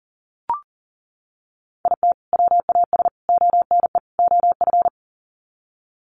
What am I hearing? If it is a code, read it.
Morse code, heard as a keyed tone, about 40 wpm, 700 Hz. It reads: IT WAS ONE OF
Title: Sets of 4 Words 40wpm